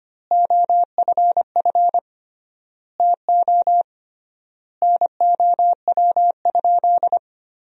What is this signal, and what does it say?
Morse code, heard as a keyed tone, about 25 wpm, 700 Hz. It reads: OFF TO NOW?
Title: 3x Word-Spacing - Sentences from Top 200 Words - 25wpm